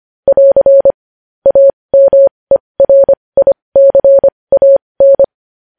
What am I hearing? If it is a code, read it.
Morse code, heard as a keyed tone, about 25 wpm, spaced as for 20 wpm, 550 Hz. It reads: <AR> AMERICAN
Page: news-headlines-in-morse-code-at-20-wpm